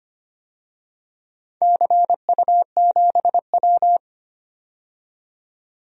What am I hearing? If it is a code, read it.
Morse code, heard as a keyed tone, about 25 wpm, 700 Hz. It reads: CU7W
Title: DXCC-100 Call Signs - 4 Characters - 25wpm